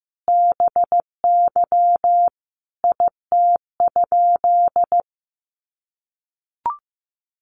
Morse code, heard as a keyed tone, about 15 wpm, 700 Hz. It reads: BY IT?
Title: Sentences from Top 100 Words 15wpm